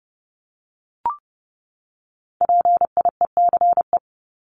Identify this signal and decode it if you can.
Morse code, heard as a keyed tone, about 30 wpm, 700 Hz. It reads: PIECE
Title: Top 500 Words 30wpm